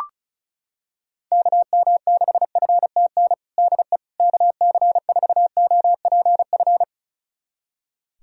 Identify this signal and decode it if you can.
Morse code, heard as a keyed tone, about 36 wpm, 700 Hz. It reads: KM6FTN DE KC4OPF
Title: USA Callsign DE Callsign 35wpm